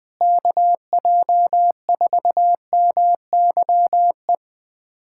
Morse code, heard as a keyed tone, about 21 wpm, 700 Hz. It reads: KJ4MYE
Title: USA Callsign DE Callsign - 6 Characters - 20wpm